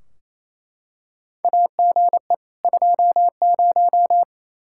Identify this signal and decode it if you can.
Morse code, heard as a keyed tone, about 27 wpm, 700 Hz. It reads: AGE 20
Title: Practice QSO - Exchange 3 - More Info - 28wpm